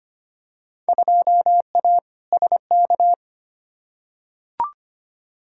Morse code, heard as a keyed tone, about 25 wpm, 700 Hz. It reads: 2A SK
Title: Field Day Exchanges Encore - 25wpm